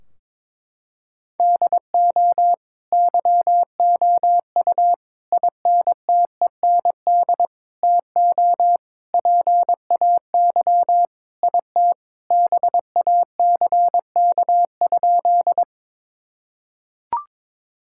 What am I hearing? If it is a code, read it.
Morse code, heard as a keyed tone, about 21 wpm, 700 Hz. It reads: DO YOU INTEND TO PAY IT BACK?
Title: Sentences from Top 1600 Words 22wpm